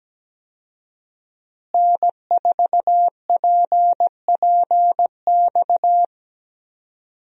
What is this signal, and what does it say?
Morse code, heard as a keyed tone, about 17 wpm, 700 Hz. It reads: N4PPX
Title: USA Call Signs - 5 Characters - Repeated 2x - 17wpm